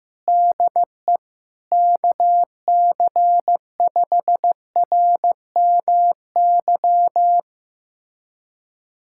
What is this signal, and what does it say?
Morse code, heard as a keyed tone, about 15 wpm, 700 Hz. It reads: DE KC5RMY
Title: USA Callsign DE Callsign 15wpm